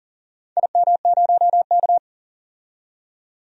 Morse code, heard as a keyed tone, about 40 wpm, 700 Hz. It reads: IM0K
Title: DXCC-100 Call Signs - 4 Characters - 40wpm